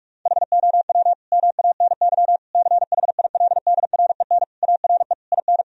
Morse code, heard as a keyed tone, about 41 wpm, 700 Hz. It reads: HOW MANY CHILDREN ARE IN
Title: Sentences from Top 1000 Words - Review Part 4 - 45wpm